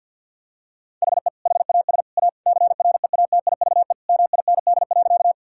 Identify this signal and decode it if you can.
Morse code, heard as a keyed tone, about 52 wpm, 700 Hz. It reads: HE HAS A CREATIVE MIND.
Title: Sentences from Top 1000 Words - Review Part 5 - 50wpm